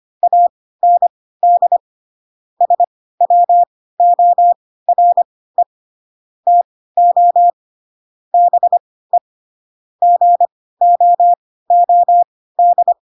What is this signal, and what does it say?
Morse code, heard as a keyed tone, about 25 wpm, spaced as for 16 wpm, 700 Hz. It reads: AND SWORE TO BE GOOD
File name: Pinocchio / Pinocchio ch2 10 25 ik1hge